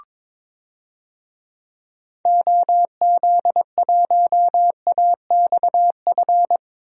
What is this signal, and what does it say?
Morse code, heard as a keyed tone, about 21 wpm, 700 Hz. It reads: OZ1AXF
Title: DXCC-100 Call Signs - 6 Characters - 22wpm